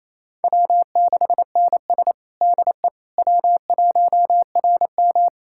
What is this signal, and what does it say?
Morse code, heard as a keyed tone, about 27 wpm, 700 Hz. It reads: W6NS DE W1RM
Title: K1USN SST QSO Practice 28wpm